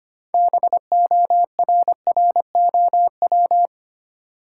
Morse code, heard as a keyed tone, about 25 wpm, 700 Hz. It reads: BORROW
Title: Sentences from Top 1500 Words - New Word Drill - 25wpm